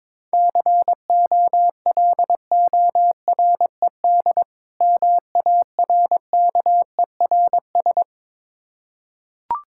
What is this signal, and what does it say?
Morse code, heard as a keyed tone, about 22 wpm, 690 Hz. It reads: COLORED MARKERS
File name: Difficult Words with Hints - 22wpm